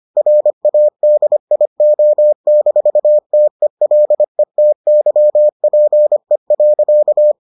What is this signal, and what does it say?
Morse code, heard as a keyed tone, about 25 wpm, 600 Hz. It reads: RADIO-TELETYPE.